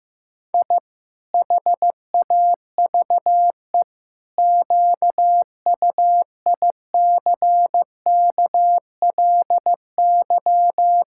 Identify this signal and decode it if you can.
Morse code, heard as a keyed tone, about 15 wpm, 700 Hz. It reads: I HAVE QUICKLY
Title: Pangram Challenge v2 - 15wpm